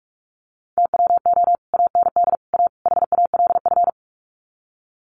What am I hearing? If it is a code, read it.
Morse code, heard as a keyed tone, about 44 wpm, 700 Hz. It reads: TWO AND A HALF
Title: Sets of 4 Words 45wpm